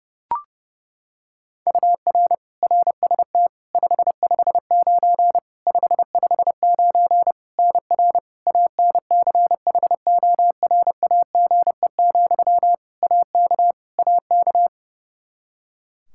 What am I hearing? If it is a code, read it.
Morse code, heard as a keyed tone, about 32 wpm, 700 Hz. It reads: UR RST 559 559 NR ANCHORAGE, AK AK
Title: Practice QSO - Exchange 1 - Beginning - 30wpm